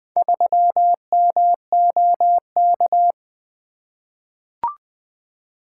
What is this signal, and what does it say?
Morse code, heard as a keyed tone, about 20 wpm, 700 Hz. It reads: 3MOK
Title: Four Letter-Number Combinations - 20wpm